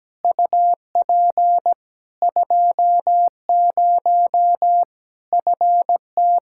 Morse code, heard as a keyed tone, about 17 wpm, 700 Hz. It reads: UP 20 FT
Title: Practice QSO - Exchange 2 - Info - 17wpm